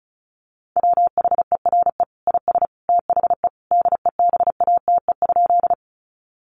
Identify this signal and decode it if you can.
Morse code, heard as a keyed tone, about 34 wpm, 700 Hz. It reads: WHERE IS THE DEBATE?
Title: Sentences from Top 1100 Words 35wpm